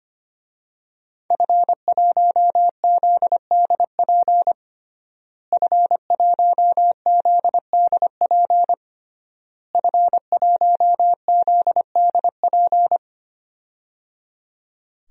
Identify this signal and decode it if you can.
Morse code, heard as a keyed tone, about 25 wpm, 700 Hz. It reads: F1ZDP F1ZDP F1ZDP
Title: DXCC-100 Call Signs - 5 Characters - Repeated 3x - 25wpm